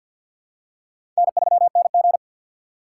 Morse code, heard as a keyed tone, about 51 wpm, 700 Hz. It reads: N3NG
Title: USA Call Signs - 4 Characters - 3x Character Spacing - 50wpm